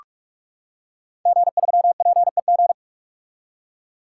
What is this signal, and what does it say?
Morse code, heard as a keyed tone, about 46 wpm, 700 Hz. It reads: G3PEG
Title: Code Groups - 5 Letter:Numbers - 45wpm